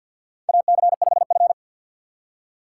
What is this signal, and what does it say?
Morse code, heard as a keyed tone, about 53 wpm, 700 Hz. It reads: AC5F
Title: USA Call Signs - 4 Characters - 3x Character Spacing - 50wpm